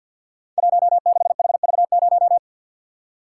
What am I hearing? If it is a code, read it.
Morse code, heard as a keyed tone, about 50 wpm, 700 Hz. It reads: 16HV0
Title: Code Groups - 5 Letter:Numbers - 50wpm